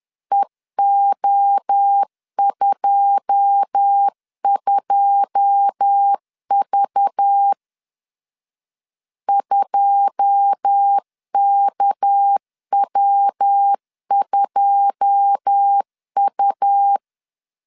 Morse code, heard as a keyed tone, about 10 wpm, 800 Hz. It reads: EO22V 2KW2U
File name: LEZIONI / 9 2 10wpm 2736